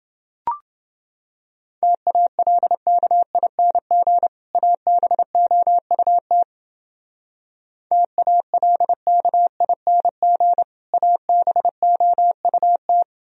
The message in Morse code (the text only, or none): TALKING ABOUT TALKING ABOUT